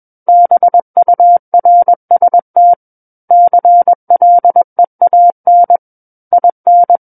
Morse code, heard as a keyed tone, about 21 wpm, 700 Hz. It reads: BURST CLEAN IN
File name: Book One / 12-Chapter 12 - 21 WPM